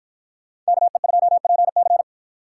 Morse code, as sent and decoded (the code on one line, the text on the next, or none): -.- . ..--- .--. -.-.
KE2PC